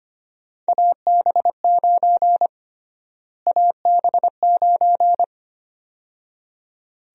Morse code, heard as a keyed tone, about 25 wpm, 700 Hz. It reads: AB9 AB9
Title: USA Call Sign Prefixes - Repeated 2x - 25wpm